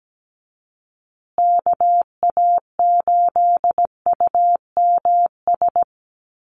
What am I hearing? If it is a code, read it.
Morse code, heard as a keyed tone, about 17 wpm, 700 Hz. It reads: KA8UMS
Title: USA Call Signs Repeated 2x 17wpm